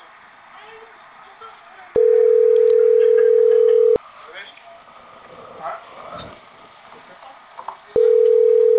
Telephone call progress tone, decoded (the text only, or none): ringback tone